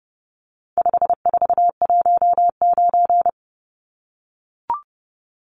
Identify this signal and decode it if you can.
Morse code, heard as a keyed tone, about 30 wpm, 700 Hz. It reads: 5419
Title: Numbers - 4-Digit 30wpm